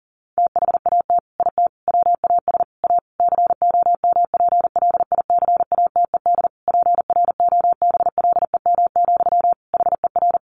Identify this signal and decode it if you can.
Morse code, heard as a keyed tone, about 40 wpm, 700 Hz. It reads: THAT IT WAS A COMPLICATED PROBLEM, HER
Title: Sentences from Top 1200 Words 40wpm